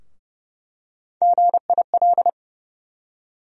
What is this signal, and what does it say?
Morse code, heard as a keyed tone, about 32 wpm, 700 Hz. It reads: GIL
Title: USA Call Sign Postfixes - 3 Letters 30wpm